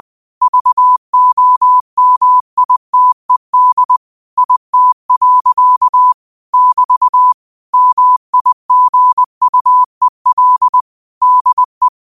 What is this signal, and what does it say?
Morse code, heard as a keyed tone, about 20 wpm, 1000 Hz. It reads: VOMITED IT. <BT> MIGUEL DE
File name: qotdfiles / 20wpm